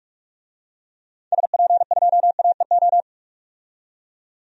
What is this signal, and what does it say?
Morse code, heard as a keyed tone, about 46 wpm, 700 Hz. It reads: SP2AEO
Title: DXCC-100 Call Signs - 6 Characters - 45wpm